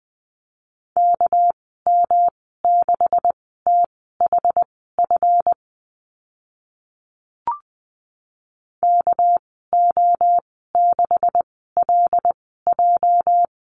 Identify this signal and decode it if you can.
Morse code, heard as a keyed tone, about 20 wpm, spaced as for 14 wpm, 700 Hz. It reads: KM6THF KO6LJ
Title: 213 Lesson 34g - Number 8 - Call Signs 20wpm